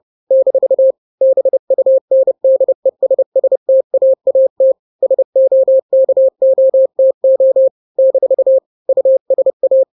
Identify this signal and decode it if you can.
Morse code, heard as a keyed tone, about 30 wpm, 530 Hz. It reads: - BUNDESSTAAT SOKOTO - USA
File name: daily / 29wpm